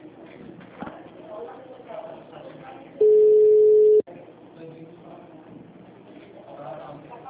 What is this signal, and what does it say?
Ringback tone (Europe, 425 Hz), with speech.
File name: out- -2030- - - .7403